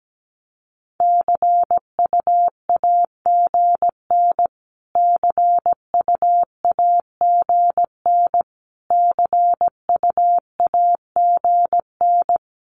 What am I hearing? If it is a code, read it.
Morse code, heard as a keyed tone, about 17 wpm, 700 Hz. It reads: CUAGN CUAGN CUAGN
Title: Abbreviations Repeated 3x - 17wpm